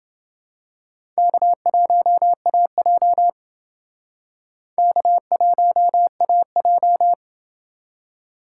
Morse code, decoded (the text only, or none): K1AJ K1AJ